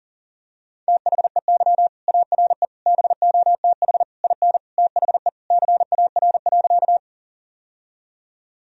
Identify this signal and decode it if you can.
Morse code, heard as a keyed tone, about 40 wpm, 700 Hz. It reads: THEY ARE BOTH IN THE CAR.
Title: Sentences from Top 1000 Words - Review Part 1 - 40wpm